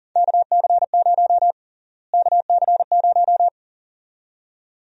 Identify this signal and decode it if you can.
Morse code, heard as a keyed tone, about 40 wpm, 700 Hz. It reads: KC0 KC0
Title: USA Call Sign Prefixes - Repeated 2x - 40wpm